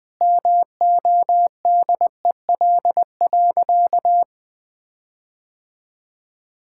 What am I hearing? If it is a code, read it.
Morse code, heard as a keyed tone, about 21 wpm, 700 Hz. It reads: MODEL.
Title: Sentences with 7-Word Phrases - Spoken Beginning - 20wpm